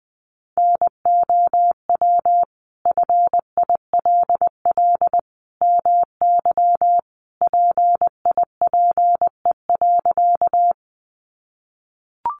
Morse code, heard as a keyed tone, about 20 wpm, 700 Hz. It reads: NOW FILL MY PIPE.
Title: Sentences from Top 1500 Words 20wpm